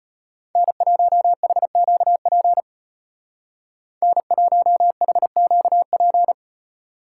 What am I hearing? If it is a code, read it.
Morse code, heard as a keyed tone, about 33 wpm, 700 Hz. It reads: N1HQP N1HQP